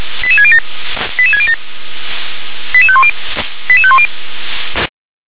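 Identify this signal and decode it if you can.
Five-tone selective call (ZVEI), heard as quick runs of five tones, tones of about 70 ms. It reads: 88607 88607 88510 88510
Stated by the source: ZVEI five-tone call received by SDR.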